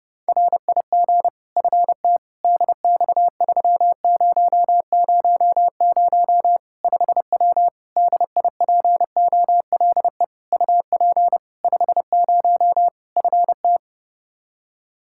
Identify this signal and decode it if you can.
Morse code, heard as a keyed tone, about 32 wpm, 700 Hz. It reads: RIG FT DX3000 5W DIPOLE UP 50 FT
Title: Practice QSO - Exchange 2 - Info - 30wpm